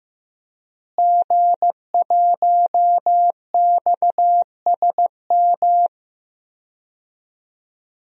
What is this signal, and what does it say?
Morse code, heard as a keyed tone, about 15 wpm, 700 Hz. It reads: G1XSM
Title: DXCC-100 Call Signs - 5 Characters - Repeated 2x - 15wpm